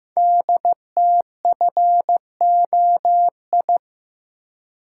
Morse code, heard as a keyed tone, about 15 wpm, 700 Hz. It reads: DTFOI
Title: Code Groups - 5 Letters - 15wpm